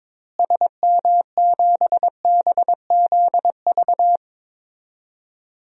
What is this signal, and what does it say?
Morse code, heard as a keyed tone, about 22 wpm, 700 Hz. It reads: SM7BZV
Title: DXCC-100 Call Signs - 6 Characters - Repeated 3x - 22wpm